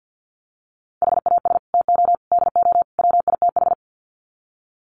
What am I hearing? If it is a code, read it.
Morse code, heard as a keyed tone, about 49 wpm, 700 Hz. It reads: HAS TO DO WITH
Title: Sets of 4 Words 50wpm